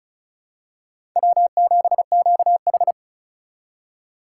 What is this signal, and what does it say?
Morse code, heard as a keyed tone, about 36 wpm, 700 Hz. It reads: W7QH
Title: USA Call Signs - 4 Characters - 2x Character Spacing - 35wpm